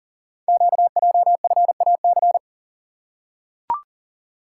Morse code, heard as a keyed tone, about 39 wpm, 690 Hz. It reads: QJFAC
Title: Code Groups - 5 Letters - 40wpm